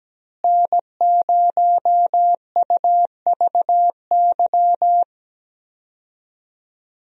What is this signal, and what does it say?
Morse code, heard as a keyed tone, about 17 wpm, 700 Hz. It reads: N0UVY
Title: USA Call Signs - 5 Characters - Repeated 2x - 17wpm